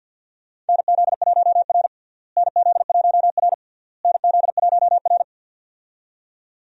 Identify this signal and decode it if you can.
Morse code, heard as a keyed tone, about 50 wpm, 700 Hz. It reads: NZ1R NZ1R NZ1R
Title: USA Call Signs - 4 Characters - Repeated 3x - 50wpm